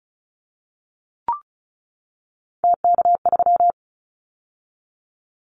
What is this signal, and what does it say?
Morse code, heard as a keyed tone, about 34 wpm, 700 Hz. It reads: TK3
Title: Callsign Prefixes for Top 100 DXCC - 35wpm